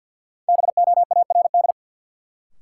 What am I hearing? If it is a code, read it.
Morse code, heard as a keyed tone, about 51 wpm, 700 Hz. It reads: BOARD
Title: Sentences from Top 700 Words - New Word Drill - 50wpm